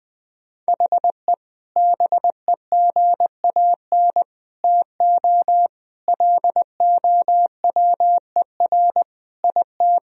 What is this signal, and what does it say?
Morse code, heard as a keyed tone, about 20 wpm, 700 Hz. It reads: HE BEGAN TO LOWER IT
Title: Sentences from Top 1000 Words - Review Part 4 - 20wpm